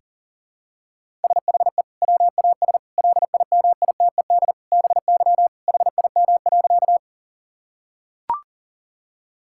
Morse code, heard as a keyed tone, about 40 wpm, 700 Hz. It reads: SHE WAS LIMITED BY HIM.
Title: Sentences from Top 1000 Words - Review Part 6 - 40wpm